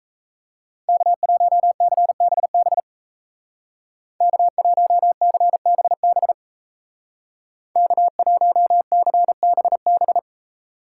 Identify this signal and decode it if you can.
Morse code, heard as a keyed tone, about 37 wpm, 700 Hz. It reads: K1CBB K1CBB K1CBB